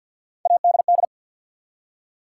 Morse code, heard as a keyed tone, about 53 wpm, 700 Hz. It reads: ADD
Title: Three Letter Words - 50wpm